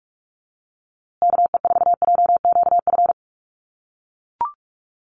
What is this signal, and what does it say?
Morse code, heard as a keyed tone, about 44 wpm, 700 Hz. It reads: KE4JQF
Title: USA Call Signs Repeated 3x 45wpm